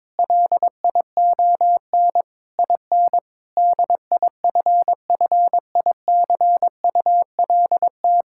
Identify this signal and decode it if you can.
Morse code, heard as a keyed tone, about 22 wpm, 700 Hz. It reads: LION IN DIFFICULT
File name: Difficult Sentences with Hints - 22wpm